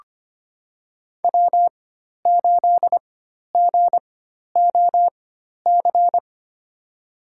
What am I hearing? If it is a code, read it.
Morse code, heard as a keyed tone, about 25 wpm, spaced as for 12 wpm, 700 Hz. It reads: W8GOC
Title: USA Call Signs - 5 Characters - 4x Character Spacing - 25wpm